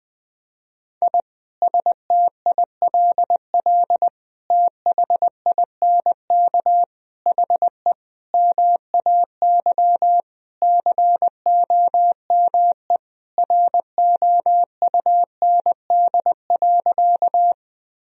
Morse code, heard as a keyed tone, about 20 wpm, 700 Hz. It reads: I STILL THINK HE MAY COME ROUND.
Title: Sentences from Top 1000 Words - Review Part 1 - 20wpm